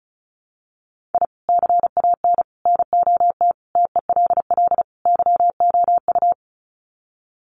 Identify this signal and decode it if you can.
Morse code, heard as a keyed tone, about 34 wpm, 700 Hz. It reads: I CAN NOT TELL YOU
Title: Sets of 5 Words 35wpm